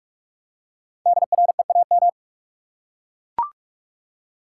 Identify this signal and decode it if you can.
Morse code, heard as a keyed tone, about 42 wpm, 700 Hz. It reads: DREAM
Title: Five Letter Words - 45wpm